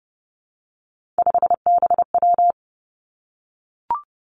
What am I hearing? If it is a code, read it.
Morse code, heard as a keyed tone, about 30 wpm, 700 Hz. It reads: 5BW
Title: Three Challenging Letter-Number Combinations 30wpm